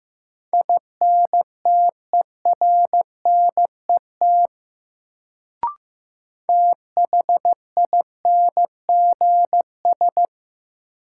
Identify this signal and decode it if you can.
Morse code, heard as a keyed tone, about 15 wpm, 700 Hz. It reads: INTERNET THINGS
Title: Sentences from Top 1000 Words - Review Part 4 - New Word Drill - 15wpm